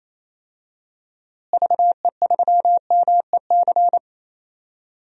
Morse code, heard as a keyed tone, about 28 wpm, 700 Hz. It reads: VE3MEC
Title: DXCC-100 Call Signs - 6 Characters - Repeated 3x - 28wpm